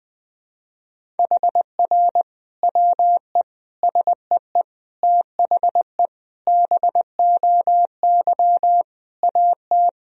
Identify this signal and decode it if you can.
Morse code, heard as a keyed tone, about 21 wpm, 700 Hz. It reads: HR WE SEE THE BOY AT
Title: Sentences from Top 300 Words and Abbreviations 20wpm